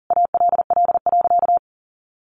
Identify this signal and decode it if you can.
Morse code, heard as a keyed tone, about 40 wpm, 700 Hz. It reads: ALL.
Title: Sentences from Top 1100 Words 40wpm